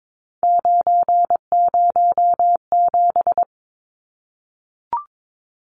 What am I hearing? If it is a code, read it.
Morse code, heard as a keyed tone, about 22 wpm, 700 Hz. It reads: 907
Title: Numbers - 3-Digit 22wpm